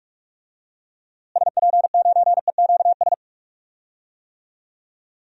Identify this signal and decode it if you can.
Morse code, heard as a keyed tone, about 41 wpm, 700 Hz. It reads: SP9EQS
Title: DXCC-100 Call Signs - 6 Characters - Repeated 3x - 45wpm